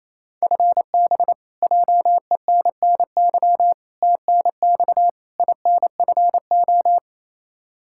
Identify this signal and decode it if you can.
Morse code, heard as a keyed tone, about 28 wpm, 700 Hz. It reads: FB JENNY TNX INFO
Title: K1USN SST QSO Practice 28wpm